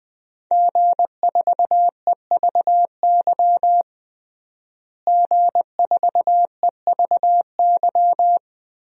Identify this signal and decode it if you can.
Morse code, heard as a keyed tone, about 20 wpm, 700 Hz. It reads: G4EVY G4EVY
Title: DXCC-100 Call Signs - 5 Characters - Repeated 3x - 20wpm